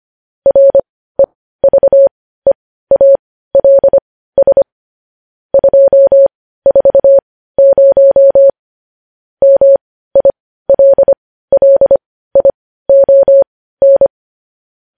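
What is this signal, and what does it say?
Morse code, heard as a keyed tone, about 25 wpm, spaced as for 15 wpm, 550 Hz. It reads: REVEALS 240 MILLION
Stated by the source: Morse code news headlines practice at 15.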